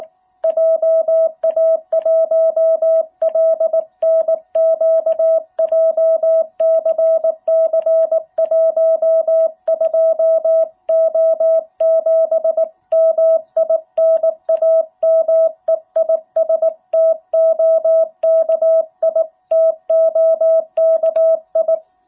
Morse code, heard as a keyed tone, about 19 wpm, 660 Hz. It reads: JA1LNQ JCC12O7 MI NAME IS TOKI TOKI
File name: jh8cbh / 2021cw